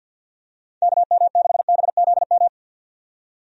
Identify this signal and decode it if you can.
Morse code, heard as a keyed tone, about 50 wpm, 700 Hz. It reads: KM6BZM
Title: USA Call Signs - 6 Characters - 50wpm